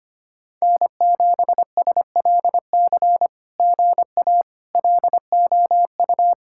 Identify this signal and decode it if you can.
Morse code, heard as a keyed tone, about 25 wpm, 700 Hz. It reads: N7SLC GA LOU
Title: SOTA QSO Practice 25wpm